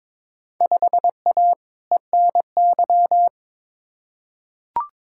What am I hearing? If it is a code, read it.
Morse code, heard as a keyed tone, about 21 wpm, 700 Hz. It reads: 5A ENY
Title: Field Day Exchanges Encore - 22wpm